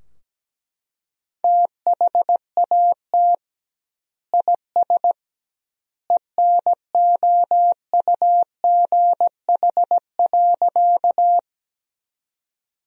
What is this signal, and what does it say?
Morse code, heard as a keyed tone, about 17 wpm, 700 Hz. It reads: THAT IS ENOUGH.
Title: 2x Word-Spacing - Sentences from Top 400 Words - 17wpm